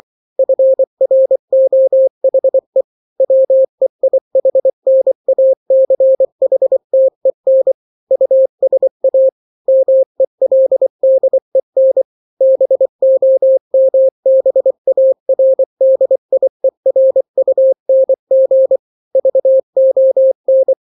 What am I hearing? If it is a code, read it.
Morse code, heard as a keyed tone, about 24 wpm, 530 Hz. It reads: FROHE WEIHNACHTEN USA MELDEN BOMBARDIERUNG VON
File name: daily / 23wpm